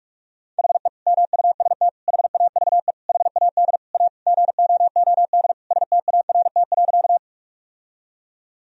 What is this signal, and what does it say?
Morse code, heard as a keyed tone, about 46 wpm, 700 Hz. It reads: HE MUST HAVE HAD A GOOD START.
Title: Sentences from Top 1000 Words - Review Part 1 - 45wpm